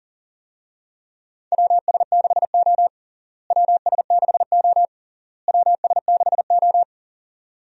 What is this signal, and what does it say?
Morse code, heard as a keyed tone, about 40 wpm, 700 Hz. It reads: WS6O WS6O WS6O
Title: USA Call Signs - 4 Characters - Repeated 3x - 40wpm